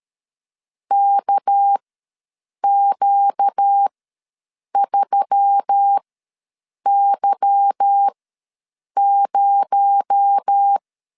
Morse code, heard as a keyed tone, about 13 wpm, spaced as for 7 wpm, 800 Hz. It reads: KQ3Y0
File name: LEZIONI / 8 YJ45 / 8 Y ws 12wpm 3409